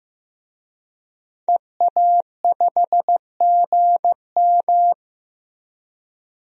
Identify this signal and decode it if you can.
Morse code, heard as a keyed tone, about 16 wpm, 700 Hz. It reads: EA5GM
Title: DXCC-100 Call Signs - 5 Characters - Repeated 2x - 15wpm